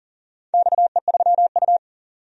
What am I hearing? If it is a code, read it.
Morse code, heard as a keyed tone, about 38 wpm, 700 Hz. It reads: XE3U
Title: DXCC-100 Call Signs - 4 Characters - 40wpm